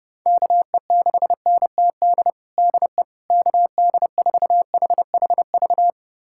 Morse code, heard as a keyed tone, about 30 wpm, 700 Hz. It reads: KE6NTD DE KD4HHV
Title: USA Callsign DE Callsign 30wpm